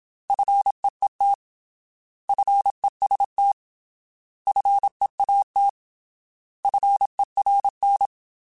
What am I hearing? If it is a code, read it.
Morse code, heard as a keyed tone, about 27 wpm, 800 Hz. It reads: FEET FEST FEAT FERN
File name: Alphabetical / Alphabetical mp3s / Fs @25